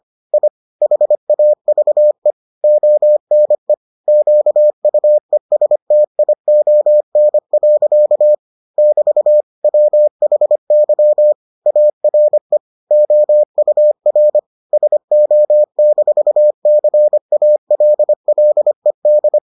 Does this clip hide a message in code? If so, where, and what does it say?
Yes: Morse code from 0.3 to 19.4 s, heard as a keyed tone, about 25 wpm, 600 Hz. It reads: I HAVE ONE QUESTION. <BT> WHY ARE OUR SO-CALLED